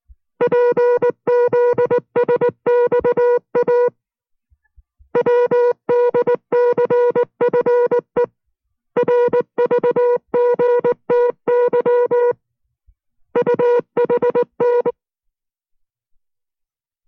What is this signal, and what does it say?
Morse code, heard as a keyed tone, about 19 wpm, 480 Hz. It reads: PZSXA WDCFE RVGTY UHN